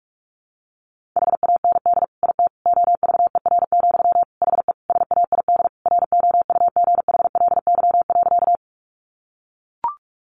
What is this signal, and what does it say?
Morse code, heard as a keyed tone, about 44 wpm, 700 Hz. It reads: HAND IT OVER, HE SAID ROUGHLY.
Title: Sentences from Top 1500 Words 45wpm